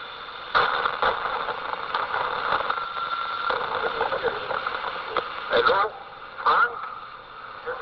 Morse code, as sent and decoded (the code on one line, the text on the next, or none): none